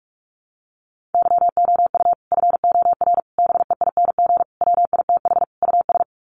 Morse code, heard as a keyed tone, about 44 wpm, 700 Hz. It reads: YOU FOR BEING WITH US
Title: Sets of 5 Words 45wpm